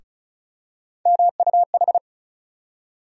Morse code, heard as a keyed tone, about 34 wpm, 700 Hz. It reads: MUH
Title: USA Call Sign Postfixes - 3 Letters 35wpm